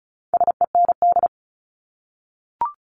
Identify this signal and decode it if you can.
Morse code, heard as a keyed tone, about 34 wpm, 690 Hz. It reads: SEND
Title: Multi-letter Phonemes 35wpm